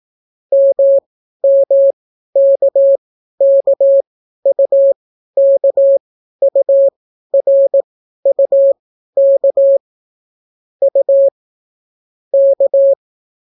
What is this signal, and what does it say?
Morse code, heard as a keyed tone, about 18 wpm, spaced as for 12 wpm, 550 Hz. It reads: MMKKUKURUK U K